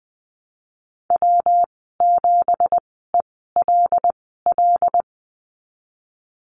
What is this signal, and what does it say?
Morse code, heard as a keyed tone, about 20 wpm, spaced as for 14 wpm, 700 Hz. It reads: W7ELL
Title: 179 Lesson 29g - Slash - Call Signs 20wpm